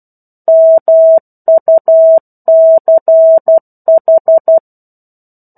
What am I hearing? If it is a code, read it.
Morse code, heard as a keyed tone, about 12 wpm, 650 Hz. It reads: MUCH
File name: 01-Chapter-01 12 WPM